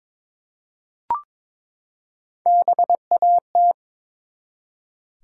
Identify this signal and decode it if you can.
Morse code, heard as a keyed tone, about 22 wpm, 700 Hz. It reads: BAT
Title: Three Letter Words - 22wpm